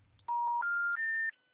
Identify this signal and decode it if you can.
Special information tone.